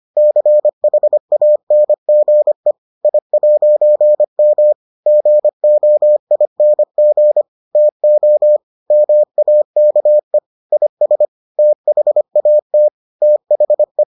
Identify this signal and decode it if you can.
Morse code, heard as a keyed tone, about 25 wpm, 600 Hz. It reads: CHANGE I'M GOING TO MAKE IS THAT THE